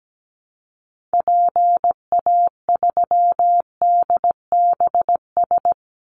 Morse code, heard as a keyed tone, about 17 wpm, 700 Hz. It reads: PA3DBS
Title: Call Signs Repeated 3x - 17wpm